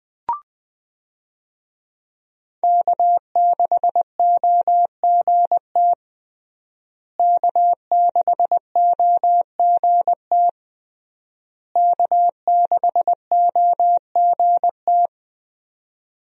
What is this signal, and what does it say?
Morse code, heard as a keyed tone, about 20 wpm, 700 Hz. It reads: K6OGT K6OGT K6OGT
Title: USA Call Signs - 5 Characters - Repeated 3x - 20wpm